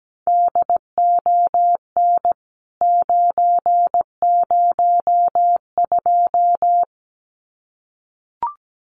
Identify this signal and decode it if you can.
Morse code, heard as a keyed tone, about 17 wpm, 700 Hz. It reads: DON 902
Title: CWT Exchange 17wpm